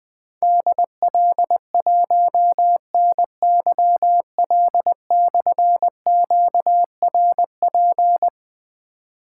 Morse code, heard as a keyed tone, about 20 wpm, 700 Hz. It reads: DL1NYL/QRP
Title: Difficult Call Signs 20wpm